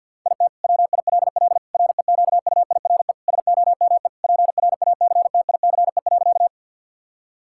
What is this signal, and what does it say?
Morse code, heard as a keyed tone, about 52 wpm, 700 Hz. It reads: IT WILL REQUIRE SOME PRACTICE.
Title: Sentences from Top 1000 Words - Review Part 6 - 50wpm